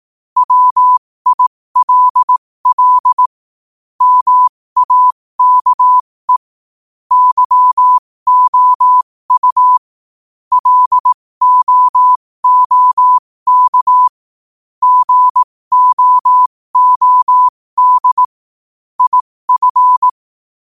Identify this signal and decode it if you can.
Morse code, heard as a keyed tone, about 18 wpm, spaced as for 15 wpm, 1000 Hz. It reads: WILL MAKE YOU LOOK GOOD IF
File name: qotdfiles / 15wpm